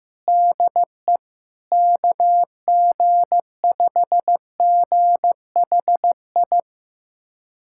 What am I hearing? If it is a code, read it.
Morse code, heard as a keyed tone, about 15 wpm, 700 Hz. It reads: DE KG5GHI
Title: USA Callsign DE Callsign 15wpm